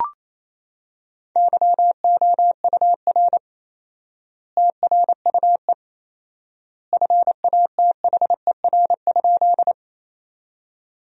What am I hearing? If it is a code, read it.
Morse code, heard as a keyed tone, about 27 wpm, 700 Hz. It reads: YOUR TRUE FATHER?
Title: 4x Word-Spacing - Sentences from Top 400 Words - 28wpm